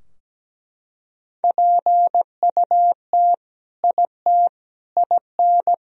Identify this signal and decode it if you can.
Morse code, heard as a keyed tone, about 17 wpm, 700 Hz. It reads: PUT IT IN
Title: Sentences from Top 1000 Words - Review Part 2 - 17wpm